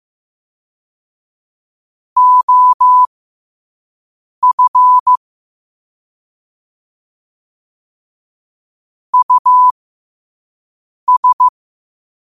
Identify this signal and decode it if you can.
Morse code, heard as a keyed tone, about 15 wpm, spaced as for 5 wpm, 1000 Hz. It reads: OF US
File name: qotdfiles / 5wpm